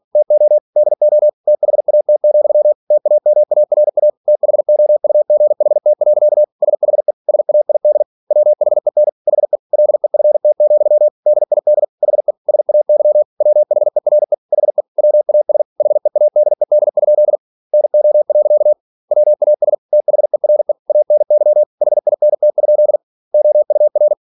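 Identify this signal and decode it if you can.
Morse code, heard as a keyed tone, about 46 wpm, 600 Hz. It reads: TO DO THAT, TAMARA THOUGHT. SHE SAID WHEN HE LEFT, DID HE SAY WHERE HE WAS HEADED? NO. WAS THERE ANY HINT? OUR
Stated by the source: Practice text from an English novel sent at 47 WPM.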